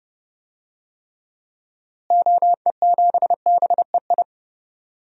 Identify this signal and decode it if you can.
Morse code, heard as a keyed tone, about 32 wpm, 700 Hz. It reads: OE7BEI
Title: DXCC-100 Call Signs - 6 Characters - Repeated 2x - 30wpm